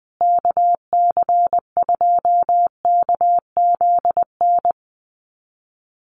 Morse code, heard as a keyed tone, about 20 wpm, 700 Hz. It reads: KC2KZN
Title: USA Call Signs Repeated 3x 20wpm